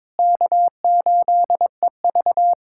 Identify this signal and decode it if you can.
Morse code, heard as a keyed tone, about 22 wpm, 700 Hz. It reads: K8EV
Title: USA Callsign DE Callsign - 4 Characters - 22wpm